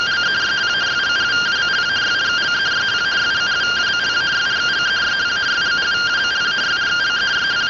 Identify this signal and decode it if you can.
Two-tone teleprinter signal, about 170 Hz shift.
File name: findings / axial iodine 3mm 205.gen-33 rtty rx